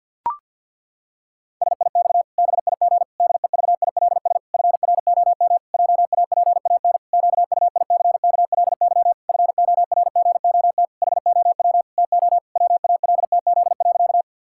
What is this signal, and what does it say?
Morse code, heard as a keyed tone, about 50 wpm, 700 Hz. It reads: SIX BIG DEVILS FROM JAPAN QUICKLY FORGOT HOW TO WALTZ.
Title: Pangram Challenge v2 - 50wpm